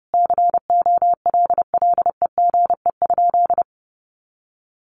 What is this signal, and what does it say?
Morse code, heard as a keyed tone, about 30 wpm, 700 Hz. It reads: COLLEGE?
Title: Sentences from Top 1300 Words 30wpm